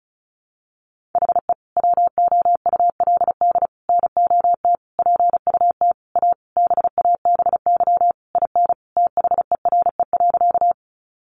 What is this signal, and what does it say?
Morse code, heard as a keyed tone, about 34 wpm, 700 Hz. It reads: HE WOULD NOT PUT A BABY IN THERE.
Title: Easy Phrases 35wpm